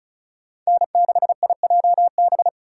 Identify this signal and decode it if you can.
Morse code, heard as a keyed tone, about 46 wpm, 700 Hz. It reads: N6IJB